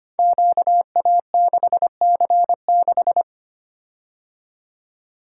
Morse code, heard as a keyed tone, about 25 wpm, 700 Hz. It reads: QA6C6
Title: Code Groups - 5 Letter:Numbers - 25wpm